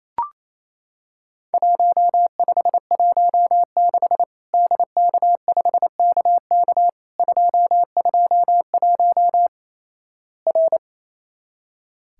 Morse code, heard as a keyed tone, about 28 wpm, 700 Hz. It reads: 1516 DK5KK 221 R
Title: Worked All Europe DX Contest - QTC Exchange - 28wpm